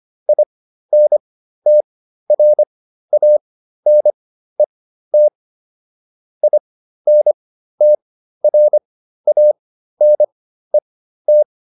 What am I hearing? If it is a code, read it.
Morse code, heard as a keyed tone, about 25 wpm, spaced as for 13 wpm, 600 Hz. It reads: INTRANET INTRANET